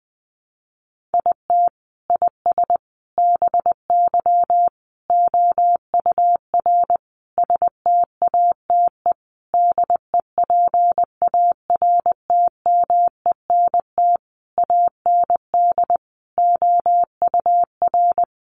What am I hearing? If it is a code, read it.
Morse code, heard as a keyed tone, about 20 wpm, 700 Hz. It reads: IT IS BY OUR STATE DEPARTMENT AND OUR
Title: Sentences from Top 1400 Words 20wpm